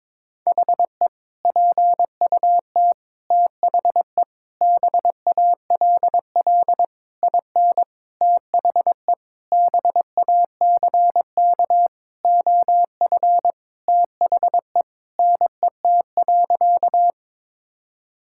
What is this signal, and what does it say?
Morse code, heard as a keyed tone, about 22 wpm, 700 Hz. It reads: HE PUT THE BALL IN THE BACK OF THE NET.
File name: practice / 22wpm / Sentences with 8-Word Phrases - 22wpm